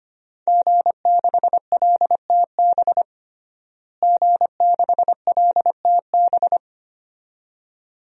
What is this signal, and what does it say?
Morse code, heard as a keyed tone, about 25 wpm, 700 Hz. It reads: G6LTB G6LTB
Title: DXCC-100 Call Signs - 5 Characters - Repeated 2x - 25wpm